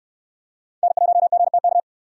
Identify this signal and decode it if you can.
Morse code, heard as a keyed tone, about 68 wpm, 700 Hz. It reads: N1GTK